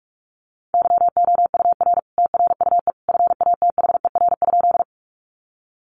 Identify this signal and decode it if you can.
Morse code, heard as a keyed tone, about 44 wpm, 700 Hz. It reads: YOUR TRUE FATHER?
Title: Sentences from Top 400 Words 45wpm